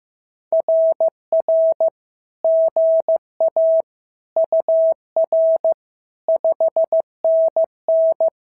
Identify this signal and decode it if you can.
Morse code, heard as a keyed tone, about 15 wpm, 650 Hz. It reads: RR GA UR 5NN
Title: SOTA QSO Practice 15wpm